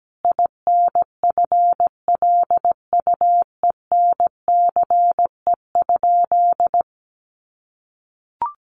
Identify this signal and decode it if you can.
Morse code, heard as a keyed tone, about 17 wpm, 700 Hz. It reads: INFLUENCE?
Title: Sentences from Top 1100 Words 17wpm